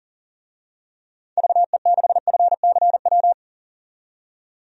Morse code, heard as a keyed tone, about 38 wpm, 700 Hz. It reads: VE6FCW
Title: DXCC-100 Call Signs - 6 Characters - 40wpm